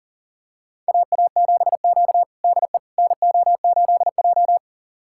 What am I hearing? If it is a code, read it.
Morse code, heard as a keyed tone, about 40 wpm, 700 Hz. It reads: AA7Q DE NO8J
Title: USA Callsign DE Callsign - 4 Characters - 40wpm